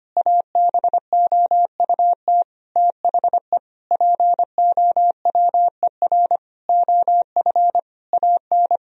Morse code, heard as a keyed tone, about 25 wpm, 700 Hz. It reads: ABOUT THE POWER OF AN
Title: Sentences from Top 1000 Words - Review Part 4 - 25wpm